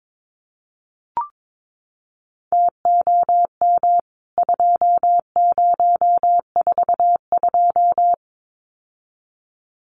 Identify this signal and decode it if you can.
Morse code, heard as a keyed tone, about 22 wpm, 700 Hz. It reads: TOM 2042
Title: CWT Exchange 22wpm